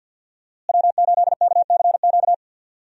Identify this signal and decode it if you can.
Morse code, heard as a keyed tone, about 49 wpm, 700 Hz. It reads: W8KCQ
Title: USA Call Signs - 5 Characters - Repeated 2x - 50wpm